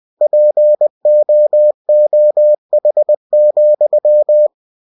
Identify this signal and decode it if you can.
Morse code, heard as a keyed tone, about 20 wpm, 600 Hz. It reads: POOH,
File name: cw-books / winnie-the-pooh / winnie-20wpm-page-0089